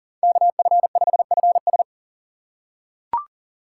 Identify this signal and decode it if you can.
Morse code, heard as a keyed tone, about 40 wpm, 700 Hz. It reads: KF5FS
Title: USA Call Signs - 3x Character Spacing - 40wpm